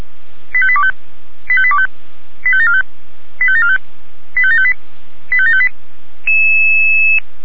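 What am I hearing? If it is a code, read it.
Five-tone selective call (ZVEI), heard as quick runs of five tones, tones of about 70 ms. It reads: 85825 85825 85745 85745 85758 85758
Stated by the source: ZVEI five-tone selective call received on an SDR.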